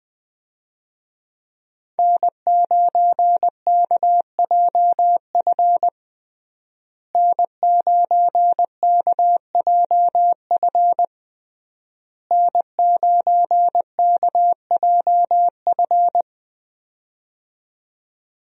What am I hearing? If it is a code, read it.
Morse code, heard as a keyed tone, about 21 wpm, 700 Hz. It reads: N9KJF N9KJF N9KJF
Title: USA Call Signs - 5 Characters - Repeated 3x - 20wpm